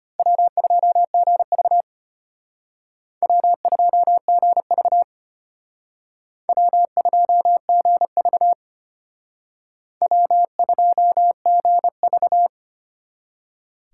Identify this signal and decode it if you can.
Morse code, heard as a keyed tone, about 32 wpm, 700 Hz. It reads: W2GV W2GV W2GV W2GV